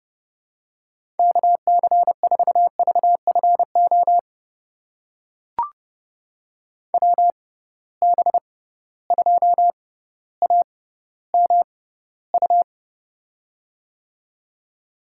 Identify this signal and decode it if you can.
Morse code, heard as a keyed tone, about 30 wpm, spaced as for 15 wpm, 700 Hz. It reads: KC4VFO WB2AMU
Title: USA Call Signs - 6 Characters - 6x Character Spacing - 30wpm